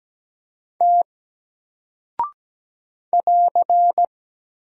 Morse code, heard as a keyed tone, about 17 wpm, 700 Hz. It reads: T <AR>
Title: Single Character (Letter:Number:Punctuation:Prosign) 17wpm